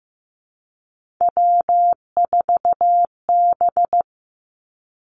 Morse code, heard as a keyed tone, about 15 wpm, 700 Hz. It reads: W4B
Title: Three Challenging Letter-Number Combinations 15wpm